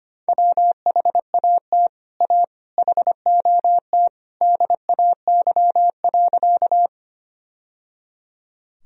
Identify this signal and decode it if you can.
Morse code, heard as a keyed tone, about 25 wpm, 700 Hz. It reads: WHAT A HOT DAY.
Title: Sentences from Top 1000 Words - Review Part 3 - 25wpm